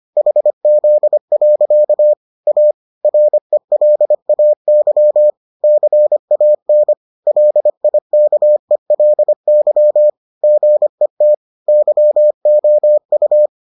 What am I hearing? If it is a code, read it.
Morse code, heard as a keyed tone, about 25 wpm, 600 Hz. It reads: HZ. A RELAY CAN LIKELY GET YOU